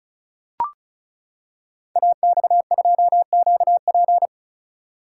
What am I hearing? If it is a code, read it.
Morse code, heard as a keyed tone, about 36 wpm, 700 Hz. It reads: AX2QP
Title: Code Groups - 5 Letter:Numbers - 35wpm